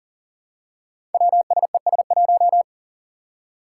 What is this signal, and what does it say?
Morse code, heard as a keyed tone, about 42 wpm, 700 Hz. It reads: WSES1
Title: Five Challenging Letter-Number Combinations 40wpm